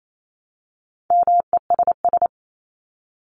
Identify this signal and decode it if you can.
Morse code, heard as a keyed tone, about 28 wpm, 700 Hz. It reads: MESS
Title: Top 1751-2000 Words 28wpm